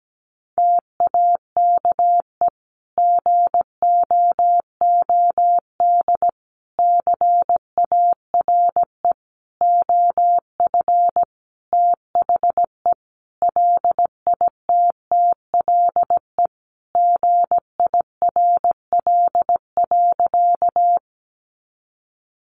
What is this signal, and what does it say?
Morse code, heard as a keyed tone, about 17 wpm, 700 Hz. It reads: TAKE GOOD CARE OF THE LITTLE GIRL.
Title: Sentences from Top 400 Words 17wpm